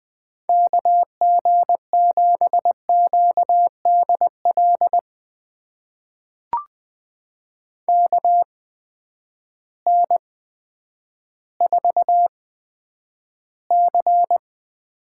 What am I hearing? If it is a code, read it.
Morse code, heard as a keyed tone, about 20 wpm, spaced as for 10 wpm, 700 Hz. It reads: KG7QDL KN4C
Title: USA Call Signs - 6 Characters - 8x Character Spacing - 20wpm